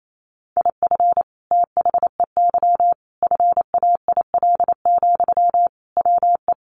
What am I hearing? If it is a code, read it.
Morse code, heard as a keyed tone, about 28 wpm, 700 Hz. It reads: IF THEY FAIL, WE
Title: Sentences from Top 1100 Words 28wpm